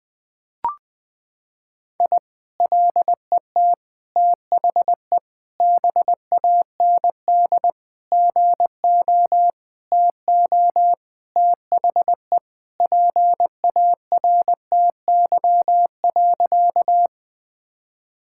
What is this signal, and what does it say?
Morse code, heard as a keyed tone, about 20 wpm, 700 Hz. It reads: I LET THE BAND GO TO THE PARTY.
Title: Sentences from Top 1600 Words 20wpm